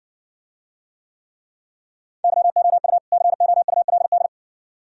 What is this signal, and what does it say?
Morse code, heard as a keyed tone, about 59 wpm, 700 Hz. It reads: YOU COULD